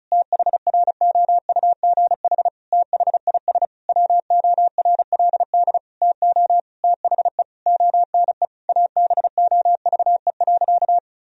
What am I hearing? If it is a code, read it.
Morse code, heard as a keyed tone, about 36 wpm, 700 Hz. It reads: THROUGH THIS WORLD TO THE ONE ABOVE.
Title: Sentences from Top 1000 Words - Review Part 2 - 35wpm